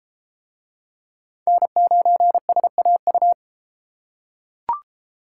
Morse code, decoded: N9SAU

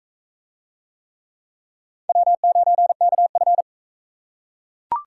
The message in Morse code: W9KF

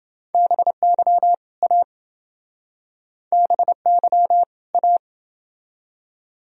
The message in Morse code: BY A BY A

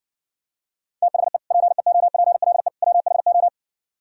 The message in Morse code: THE PEOPLE WHO

